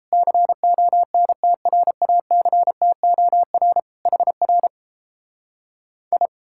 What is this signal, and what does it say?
Morse code, heard as a keyed tone, about 37 wpm, 700 Hz. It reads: CONTRACTOR HR I